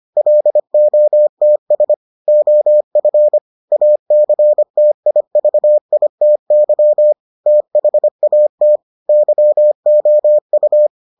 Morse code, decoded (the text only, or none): LOTS OF ACTIVITY THAT YOU